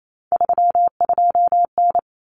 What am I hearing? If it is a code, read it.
Morse code, heard as a keyed tone, about 28 wpm, 700 Hz. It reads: 32N
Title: Common Cut Numbers - 3-Digit 28wpm